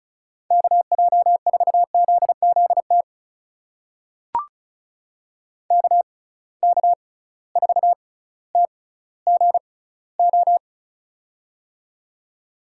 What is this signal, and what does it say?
Morse code, heard as a keyed tone, about 36 wpm, spaced as for 18 wpm, 700 Hz. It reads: KJ4ZZT KK4TGO
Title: USA Call Signs - 6 Characters - 6x Character Spacing - 35wpm